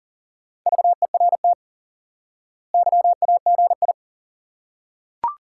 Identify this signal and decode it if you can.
Morse code, heard as a keyed tone, about 40 wpm, 700 Hz. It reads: VERT YAGI
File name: IQE Course 3 - Lesson 89 - YAGI - Two Elements - 40wpm